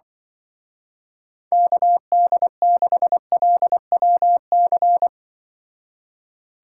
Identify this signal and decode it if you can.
Morse code, heard as a keyed tone, about 24 wpm, 700 Hz. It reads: KD6LWC
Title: Speed Racing - USA 2x3 Call Signs 20wpm